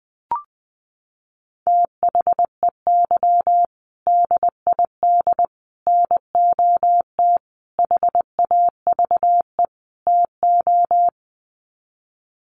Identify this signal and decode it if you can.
Morse code, heard as a keyed tone, about 20 wpm, 700 Hz. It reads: THEY DID NOT HAVE TO
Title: Sets of 5 Words Encore2 20wpm